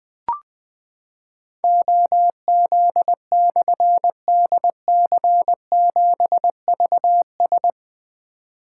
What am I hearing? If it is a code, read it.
Morse code, heard as a keyed tone, about 20 wpm, 700 Hz. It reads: OZ/DC7VS
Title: Difficult Call Signs 20wpm